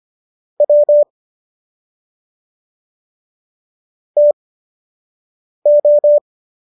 Morse code, heard as a keyed tone, about 25 wpm, spaced as for 6 wpm, 590 Hz. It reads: W TO